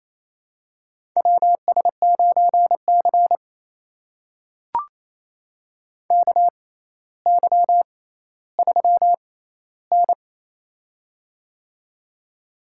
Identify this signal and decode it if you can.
Morse code, heard as a keyed tone, about 28 wpm, spaced as for 14 wpm, 700 Hz. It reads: WS9C KY3N
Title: USA Call Signs - 4 Characters - 6x Character Spacing - 28wpm